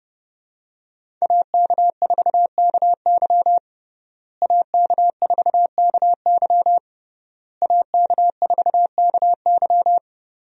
Morse code, heard as a keyed tone, about 30 wpm, 700 Hz. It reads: AK4KY AK4KY AK4KY
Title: USA Call Signs - 5 Characters - Repeated 3x - 30wpm